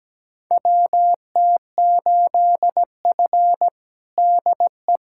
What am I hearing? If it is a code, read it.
Morse code, heard as a keyed tone, about 17 wpm, 700 Hz. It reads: WT8F DE
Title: USA Callsign DE Callsign - 4 Characters - 17wpm